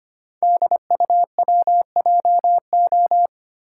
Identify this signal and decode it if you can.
Morse code, heard as a keyed tone, about 25 wpm, 700 Hz. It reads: DUWJO
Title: Code Groups - 5 Letter:Numbers - 25wpm